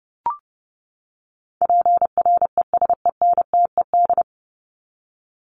Morse code, heard as a keyed tone, about 30 wpm, 700 Hz. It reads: PRESENTED
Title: ED Suffix - Top 100 Words with Base Words 30wpm